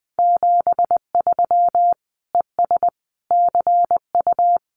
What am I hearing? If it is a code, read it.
Morse code, heard as a keyed tone, about 20 wpm, 700 Hz. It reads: 73 ES CU
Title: Common QSO Phrases Encore 20wpm